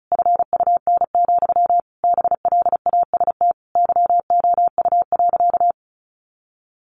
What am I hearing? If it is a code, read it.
Morse code, heard as a keyed tone, about 34 wpm, 700 Hz. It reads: FUN, BLAST YOU.
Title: A Sound Of Thunder 35wpm